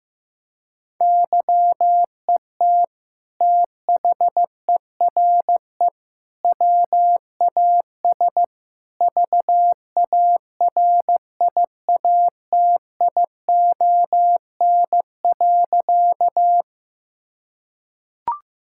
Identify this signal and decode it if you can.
Morse code, heard as a keyed tone, about 15 wpm, 700 Hz. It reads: YET THERE WAS VARIATION.
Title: Sentences from Top 1600 Words 15wpm